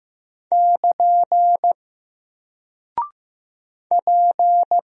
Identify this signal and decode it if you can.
Morse code, heard as a keyed tone, about 15 wpm, 700 Hz. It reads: <KN> P
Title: Single Character (Letter:Number:Punctuation:Prosign) 15wpm